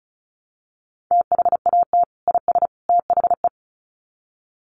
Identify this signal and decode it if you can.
Morse code, heard as a keyed tone, about 34 wpm, 700 Hz. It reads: THAT IS THE
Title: Sets of 3 Words 35wpm